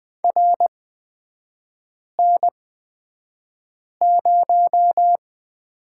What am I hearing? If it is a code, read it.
Morse code, heard as a keyed tone, about 20 wpm, spaced as for 5 wpm, 700 Hz. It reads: RN0
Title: 250 Lesson 40e - BK - Three Characters 20wpm